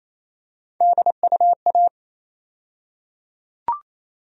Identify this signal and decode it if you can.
Morse code, heard as a keyed tone, about 28 wpm, 700 Hz. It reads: DUA
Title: USA Call Sign Postfixes - 3 Letters 28wpm